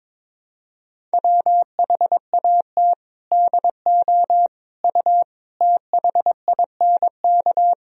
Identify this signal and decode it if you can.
Morse code, heard as a keyed tone, about 22 wpm, 700 Hz. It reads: WHAT DO U THINK
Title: Sentences from Top 300 Words and Abbreviations 22wpm